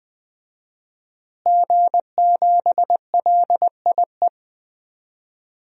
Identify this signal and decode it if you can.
Morse code, heard as a keyed tone, about 21 wpm, 700 Hz. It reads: G7LIE
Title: DXCC-100 Call Signs - 5 Characters - Repeated 2x - 20wpm